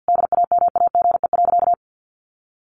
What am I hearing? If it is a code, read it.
Morse code, heard as a keyed tone, about 48 wpm, 700 Hz. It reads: DAMAGE.
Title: Sentences from Top 1100 Words 50wpm